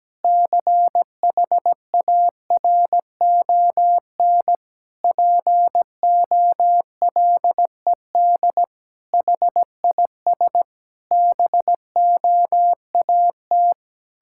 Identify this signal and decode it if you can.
Morse code, heard as a keyed tone, about 17 wpm, 700 Hz. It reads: CHARON POLED HIS BOAT
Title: Pangram Challenge v2 - 17wpm